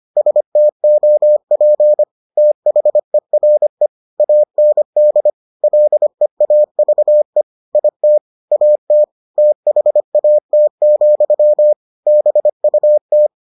Morse code, heard as a keyed tone, about 25 wpm, 600 Hz. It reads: STOP THERE AND LEAVE IT AT THAT, BUT